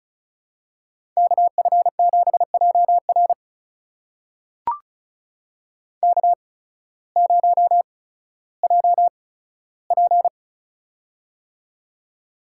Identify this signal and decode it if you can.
Morse code, heard as a keyed tone, about 36 wpm, spaced as for 16 wpm, 700 Hz. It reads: KF7JR K0JP
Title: USA Call Signs - 8x Character Spacing - 35wpm